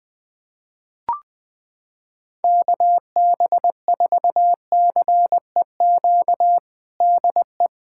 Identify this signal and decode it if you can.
Morse code, heard as a keyed tone, about 20 wpm, 700 Hz. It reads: KB4CEQ DE
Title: USA Callsign DE Callsign 20wpm